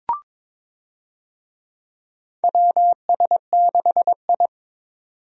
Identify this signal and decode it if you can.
Morse code, heard as a keyed tone, about 22 wpm, 700 Hz. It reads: WS6I
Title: USA Call Signs - 4 Characters - 22wpm